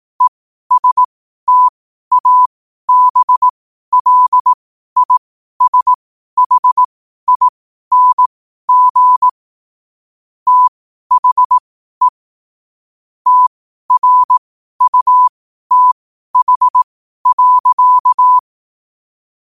Morse code, heard as a keyed tone, about 18 wpm, spaced as for 12 wpm, 1000 Hz. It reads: ESTABLISHING THE TRUTH.
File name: qotdfiles / 12wpm